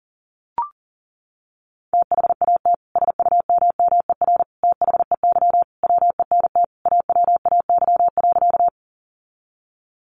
Morse code, heard as a keyed tone, about 40 wpm, 700 Hz. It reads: THAT SUMMER THEY WENT AWAY.
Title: Sentences from Top 1100 Words 40wpm